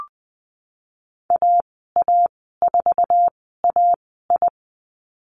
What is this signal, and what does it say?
Morse code, heard as a keyed tone, about 20 wpm, spaced as for 15 wpm, 700 Hz. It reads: AA4AI
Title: 048 Lesson 9g - Number 4 - Call Signs 20wpm